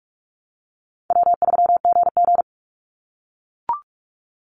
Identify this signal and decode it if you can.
Morse code, heard as a keyed tone, about 44 wpm, 700 Hz. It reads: W3GG
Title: Easy Call Signs - 45wpm